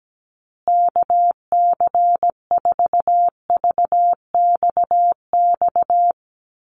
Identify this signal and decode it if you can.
Morse code, heard as a keyed tone, about 17 wpm, 700 Hz. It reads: KC4VXX
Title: USA Call Signs Repeated 2x 17wpm